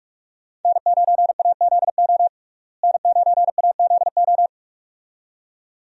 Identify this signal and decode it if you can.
Morse code, heard as a keyed tone, about 41 wpm, 700 Hz. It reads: N9AZO N9AZO
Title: USA Call Signs - 5 Characters - Repeated 2x - 45wpm